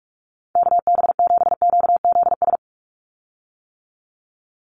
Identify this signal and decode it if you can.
Morse code, heard as a keyed tone, about 44 wpm, 700 Hz. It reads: KB7QZS
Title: USA Call Signs 45wpm